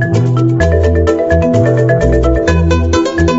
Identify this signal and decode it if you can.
There is no code.